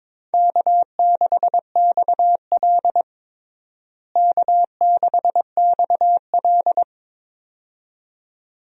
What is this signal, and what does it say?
Morse code, heard as a keyed tone, about 22 wpm, 700 Hz. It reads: K6XL K6XL
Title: USA Call Signs - 4 Characters - Repeated 2x - 22wpm